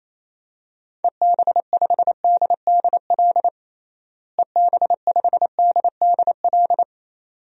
Morse code, heard as a keyed tone, about 28 wpm, 700 Hz. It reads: EB5DDL EB5DDL
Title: DXCC-100 Call Signs - 6 Characters - Repeated 2x - 28wpm